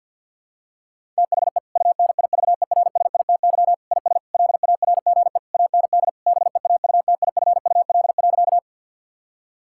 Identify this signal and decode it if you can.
Morse code, heard as a keyed tone, about 52 wpm, 700 Hz. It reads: THE UNIVERSITY IS LARGE AND BEAUTIFUL.
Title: Sentences from Top 1000 Words - Review Part 5 - 50wpm